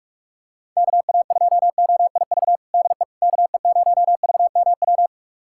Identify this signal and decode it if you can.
Morse code, heard as a keyed tone, about 43 wpm, 700 Hz. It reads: KA2OIV DE KE0VMW
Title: USA Callsign DE Callsign - 6 Characters - 45wpm